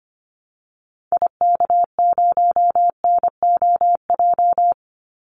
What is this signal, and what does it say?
Morse code, heard as a keyed tone, about 25 wpm, 700 Hz. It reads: IK0NOJ
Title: Call Signs Repeated 2x - 25wpm